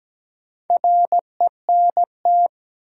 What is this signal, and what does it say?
Morse code, heard as a keyed tone, about 17 wpm, 700 Hz. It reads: RENT
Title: Sentences from Top 1000 Words - Review Part 5 - New Word Drill - 17wpm